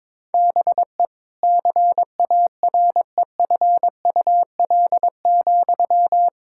Morse code, heard as a keyed tone, about 22 wpm, 700 Hz. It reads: BE CAREFUL,
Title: Sentences from Top 1600 Words 22wpm